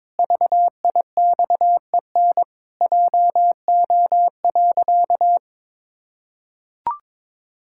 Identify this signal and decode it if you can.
Morse code, heard as a keyed tone, about 22 wpm, 700 Hz. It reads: VIXEN JO.
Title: Pangram Challenge v2 - 22wpm